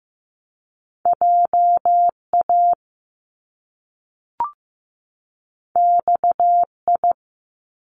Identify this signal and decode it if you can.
Morse code, heard as a keyed tone, about 15 wpm, 700 Hz. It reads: JA XI
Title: Top Two Letter Combinations in Words 15wpm